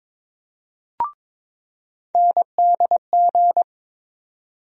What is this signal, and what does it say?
Morse code, heard as a keyed tone, about 21 wpm, 700 Hz. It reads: NDG
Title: USA Call Sign Postfixes - 3 Letters 22wpm